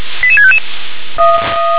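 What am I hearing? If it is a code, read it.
Five-tone selective call (ZVEI), heard as quick runs of five tones, tones of about 70 ms. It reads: 88655
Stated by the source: ZVEI five-tone selective call received by SDR.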